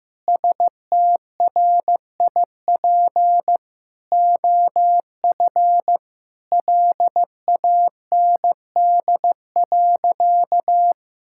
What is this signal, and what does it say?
Morse code, heard as a keyed tone, about 15 wpm, 700 Hz. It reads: STRIP OF LAND.
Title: Sentences from Top 1600 Words 15wpm